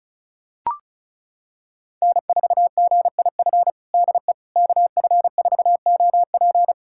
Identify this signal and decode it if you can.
Morse code, heard as a keyed tone, about 36 wpm, 700 Hz. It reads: N4GIF DE KF4OP
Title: USA Callsign DE Callsign - 5 Characters - 35wpm